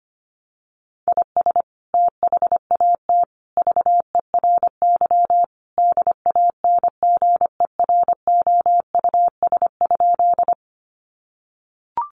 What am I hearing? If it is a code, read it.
Morse code, heard as a keyed tone, about 25 wpm, 700 Hz. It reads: IS THAT VERY DANGEROUS?
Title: Sentences from Top 1200 Words 25wpm